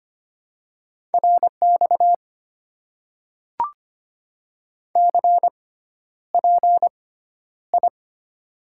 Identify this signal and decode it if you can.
Morse code, heard as a keyed tone, about 25 wpm, spaced as for 11 wpm, 690 Hz. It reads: RX CPI
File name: Top 90 QSO Elements - 6x Character Spacing - 25wpm